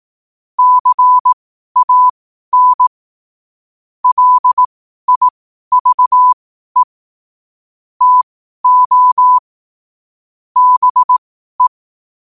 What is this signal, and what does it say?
Morse code, heard as a keyed tone, about 18 wpm, spaced as for 12 wpm, 1000 Hz. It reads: CAN LIVE TO BE
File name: qotdfiles / 12wpm